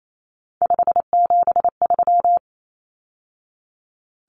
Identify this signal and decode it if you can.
Morse code, heard as a keyed tone, about 28 wpm, 700 Hz. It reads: 573
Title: Numbers - 3-Digit 28wpm